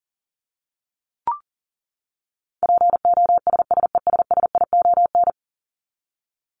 Morse code, heard as a keyed tone, about 40 wpm, 700 Hz. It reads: POSSESSION
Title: Top 1251-1500 Words 40wpm